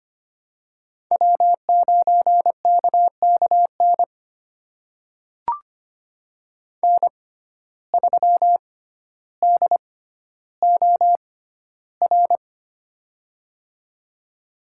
Morse code, heard as a keyed tone, about 25 wpm, spaced as for 13 wpm, 700 Hz. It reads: W9KKN N3DOR
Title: USA Call Signs - 5 Characters - 6x Character Spacing - 25wpm